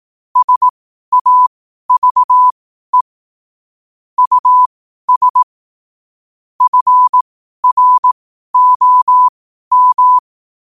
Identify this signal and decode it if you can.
Morse code, heard as a keyed tone, about 18 wpm, spaced as for 12 wpm, 1000 Hz. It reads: SAVE US FROM
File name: qotdfiles / 12wpm